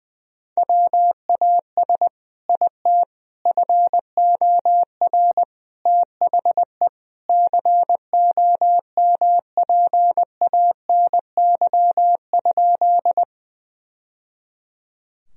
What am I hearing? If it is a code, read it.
Morse code, heard as a keyed tone, about 19 wpm, 700 Hz. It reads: WAS IT FOR THE COMPANY?
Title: Sentences from Top 1000 Words - Review Part 3 - 20wpm